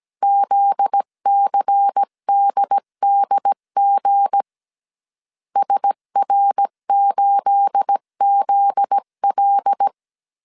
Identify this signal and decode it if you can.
Morse code, heard as a keyed tone, about 17 wpm, 800 Hz. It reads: ZCDDG SR8ZL
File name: CW / DET / 1 9 1500C S B 16wpm 2856